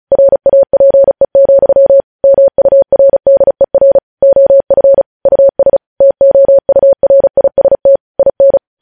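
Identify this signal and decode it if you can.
Morse code, heard as a keyed tone, about 36 wpm, 550 Hz. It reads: RAPE, MURDER OF US TOURIST IN
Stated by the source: News headlines sent in Morse at 35 wpm.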